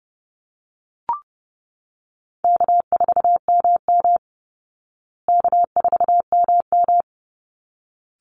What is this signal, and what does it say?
Morse code, heard as a keyed tone, about 30 wpm, 700 Hz. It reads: K4MM K4MM
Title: Easy Call Signs Repeated 2x - 30wpm